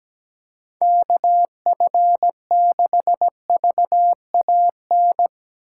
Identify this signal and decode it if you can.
Morse code, heard as a keyed tone, about 17 wpm, 700 Hz. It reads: KF6VAN
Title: USA Call Signs - 6 Characters - 8x Character Spacing - 17wpm